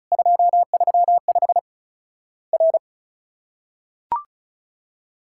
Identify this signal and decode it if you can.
Morse code, heard as a keyed tone, about 36 wpm, 700 Hz. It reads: 235 R
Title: Worked All Europe DX Contest - QTC Exchange - 35wpm